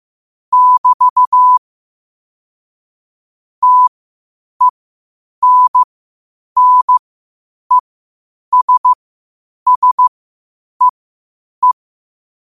Morse code, heard as a keyed tone, about 15 wpm, spaced as for 8 wpm, 1000 Hz. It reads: <BT> TENNESSEE
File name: qotdfiles / 8wpm